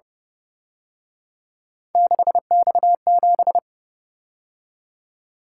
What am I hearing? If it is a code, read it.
Morse code, heard as a keyed tone, about 30 wpm, 700 Hz. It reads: 6X7
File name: Speed-Racing - Three Letter-Number Combinations - 25wpm